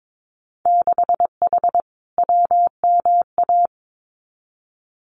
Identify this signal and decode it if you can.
Morse code, heard as a keyed tone, about 22 wpm, 700 Hz. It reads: 6H WMA
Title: Winter Field Day Exchanges Encore 22wpm